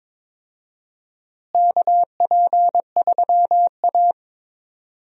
Morse code, heard as a keyed tone, about 22 wpm, 700 Hz. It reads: KP3A
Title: USA Call Signs - 4 Characters - Repeated 2x - 22wpm